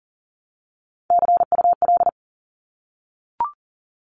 Morse code, heard as a keyed tone, about 40 wpm, 700 Hz. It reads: CUL
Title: Abbreviations Repeated 3x - 40wpm